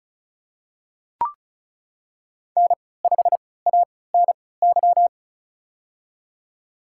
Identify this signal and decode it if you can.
Morse code, heard as a keyed tone, about 36 wpm, spaced as for 20 wpm, 700 Hz. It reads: N5ANY
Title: USA Call Signs - 5 Characters - 3x Character Spacing - 35wpm